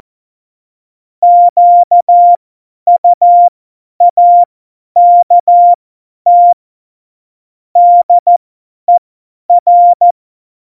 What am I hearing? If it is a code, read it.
Morse code, heard as a keyed tone, about 14 wpm, spaced as for 10 wpm, 700 Hz. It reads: QUAKT DER
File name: Morsekurs / Witze / Witz 27 March 2018, 50 F70